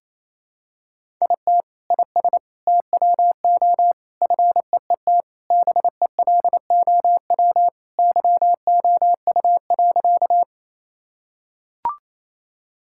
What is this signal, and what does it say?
Morse code, heard as a keyed tone, about 28 wpm, 700 Hz. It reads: IT IS TWO FEET BELOW YOU.
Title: Sentences from Top 1000 Words - Review Part 5 - 28wpm